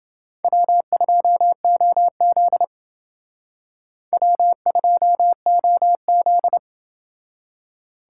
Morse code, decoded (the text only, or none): W2OZ W2OZ